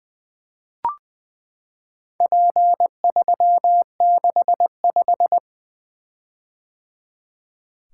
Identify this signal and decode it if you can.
Morse code, heard as a keyed tone, about 20 wpm, 700 Hz. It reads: P365
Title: Four Letter-Number Combinations - 20wpm